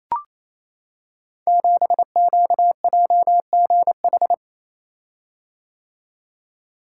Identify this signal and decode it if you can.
Morse code, heard as a keyed tone, about 29 wpm, 700 Hz. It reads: 7QJGH
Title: Code Groups - 5 Letter:Numbers - 28wpm